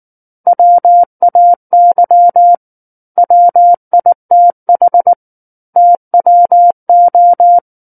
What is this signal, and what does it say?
Morse code, heard as a keyed tone, about 19 wpm, 700 Hz. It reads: WAY WITH TWO
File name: Book One / 10-Chapter 10 - 19 WPM